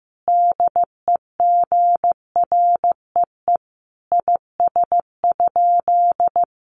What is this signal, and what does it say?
Morse code, heard as a keyed tone, about 15 wpm, 700 Hz. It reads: DEGREE IS?
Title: Sentences from Top 1500 Words 15wpm